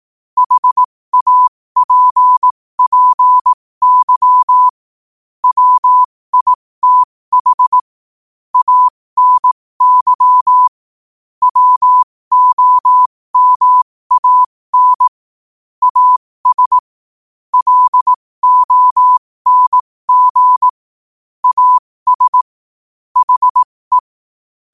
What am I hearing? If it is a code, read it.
Morse code, heard as a keyed tone, about 18 wpm, spaced as for 15 wpm, 1000 Hz. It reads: HAPPY WITH ANY WOMAN AS LONG AS HE
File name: qotdfiles / 15wpm